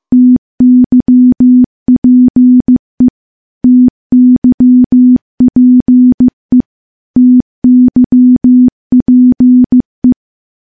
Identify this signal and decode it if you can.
Morse code, heard as a keyed tone, about 15 wpm, 250 Hz. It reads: TYPE TYPE TYPE
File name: morse-training-15WPM